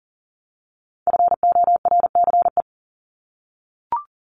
Morse code, heard as a keyed tone, about 40 wpm, 700 Hz. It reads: FORCE
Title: ED Suffix - Top 100 Words with Base Words 40wpm